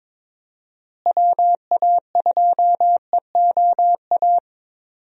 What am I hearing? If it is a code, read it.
Morse code, heard as a keyed tone, about 22 wpm, 700 Hz. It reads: WA2EOA
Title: USA Call Signs - 6 Characters - Repeated 3x - 22wpm